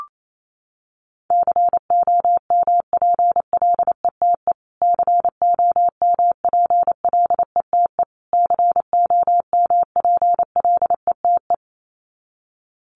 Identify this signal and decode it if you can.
Morse code, heard as a keyed tone, about 28 wpm, 700 Hz. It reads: COMPLETE COMPLETE COMPLETE
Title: Top 500 Words Repeated 28wpm